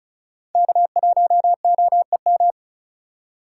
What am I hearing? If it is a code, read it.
Morse code, heard as a keyed tone, about 36 wpm, 700 Hz. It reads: K1OEM
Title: USA Call Signs - 5 Characters - 3x Character Spacing - 35wpm